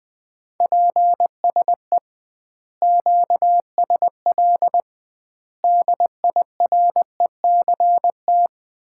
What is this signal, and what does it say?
Morse code, heard as a keyed tone, about 20 wpm, 700 Hz. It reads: PSE QSL DIRECT
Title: Common QSO Phrases - 2x Word-Spacing - 20wpm